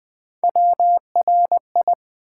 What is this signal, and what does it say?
Morse code, heard as a keyed tone, about 20 wpm, 700 Hz. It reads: WRI
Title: Build A Word - 3 Letters At A Time - 6-Letter Words - 20wpm